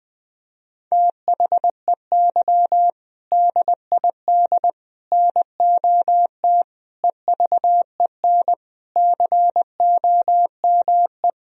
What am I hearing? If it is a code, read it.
Morse code, heard as a keyed tone, about 20 wpm, 700 Hz. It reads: THEY DID NOT EVEN COME
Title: Sentences from Top 1000 Words - Review Part 1 - 20wpm